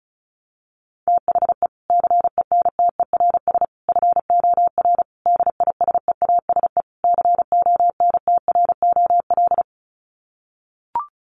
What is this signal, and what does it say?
Morse code, heard as a keyed tone, about 34 wpm, 700 Hz. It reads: THE CENTERS FOR DISEASE CONTROL
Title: Sets of 5 Words Encore2 35wpm